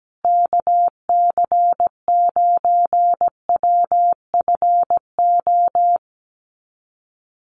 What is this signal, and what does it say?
Morse code, heard as a keyed tone, about 17 wpm, 700 Hz. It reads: KC9WFO
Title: USA Call Signs Repeated 3x 17wpm